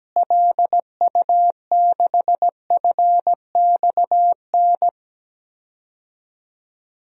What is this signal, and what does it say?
Morse code, heard as a keyed tone, about 17 wpm, 700 Hz. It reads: LU6FXN
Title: DXCC-100 Call Signs - 6 Characters - Repeated 2x - 17wpm